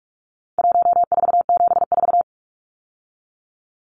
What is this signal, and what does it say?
Morse code, heard as a keyed tone, about 44 wpm, 700 Hz. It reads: 1474
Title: Numbers - 4-Digit 45wpm